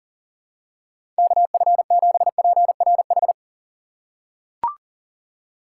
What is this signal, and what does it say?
Morse code, heard as a keyed tone, about 40 wpm, 700 Hz. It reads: KF7PRH
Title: USA Call Signs - 6 Characters - 2x Character Spacing - 40wpm